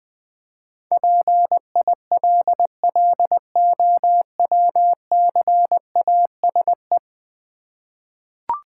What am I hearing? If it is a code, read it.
Morse code, heard as a keyed tone, about 20 wpm, 700 Hz. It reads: PILLOWCASE
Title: Easy Words with Hints - 20wpm